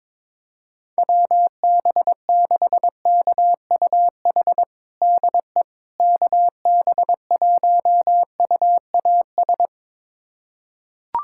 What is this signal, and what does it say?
Morse code, heard as a keyed tone, about 22 wpm, 700 Hz. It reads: WB6KUH DE KB1UAS
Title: USA Callsign DE Callsign - 6 Characters - 22wpm